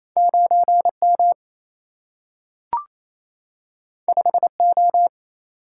Morse code, heard as a keyed tone, about 28 wpm, 700 Hz. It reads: 9 M 5 O
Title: Single-Letter Single-Number 28wpm